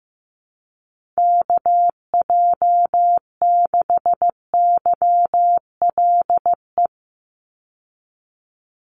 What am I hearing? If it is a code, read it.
Morse code, heard as a keyed tone, about 15 wpm, 700 Hz. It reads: KJ6YLE
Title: USA Call Signs 15wpm